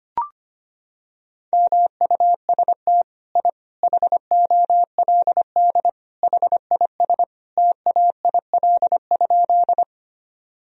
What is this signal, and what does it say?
Morse code, heard as a keyed tone, about 25 wpm, 700 Hz. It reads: MUST I HOLD HIS TAIL?
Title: Sentences from Top 1000 Words - Review Part 3 - 25wpm